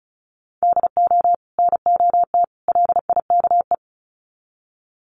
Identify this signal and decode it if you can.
Morse code, heard as a keyed tone, about 34 wpm, 700 Hz. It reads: DO NOT LIKE
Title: Sets of 3 Words 35wpm